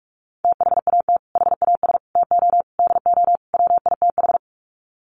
Morse code, heard as a keyed tone, about 44 wpm, 700 Hz. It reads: THAT HAS TO DO WITH
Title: Sets of 5 Words Encore2 45wpm